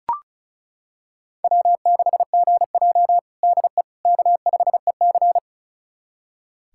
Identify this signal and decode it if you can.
Morse code, heard as a keyed tone, about 36 wpm, 700 Hz. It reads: W6GJ DE K5EC
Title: USA Callsign DE Callsign - 4 Characters - 35wpm